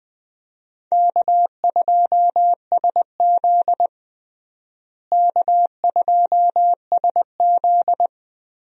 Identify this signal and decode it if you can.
Morse code, heard as a keyed tone, about 20 wpm, 700 Hz. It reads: K2SZ K2SZ
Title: USA Call Signs - 4 Characters - Repeated 2x - 20wpm